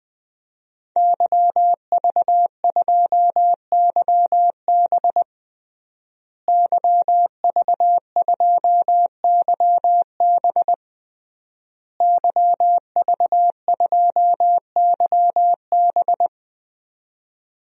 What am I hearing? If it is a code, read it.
Morse code, heard as a keyed tone, about 21 wpm, 700 Hz. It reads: YV2YB YV2YB YV2YB
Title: DXCC-100 Call Signs - 5 Characters - Repeated 3x - 20wpm